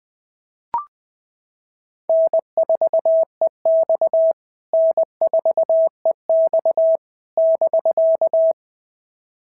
Morse code, heard as a keyed tone, about 20 wpm, 690 Hz. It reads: N4EX N4EX <BK>
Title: POTA QSO Practice - 20wpm